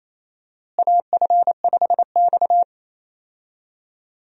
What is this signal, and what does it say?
Morse code, heard as a keyed tone, about 27 wpm, 700 Hz. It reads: AF5X
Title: USA Call Signs - 4 Characters - Repeated 3x - 28wpm